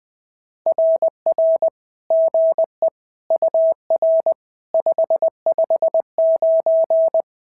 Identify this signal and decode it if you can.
Morse code, heard as a keyed tone, about 20 wpm, 650 Hz. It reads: RR GE UR 559
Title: SOTA QSO Practice 20wpm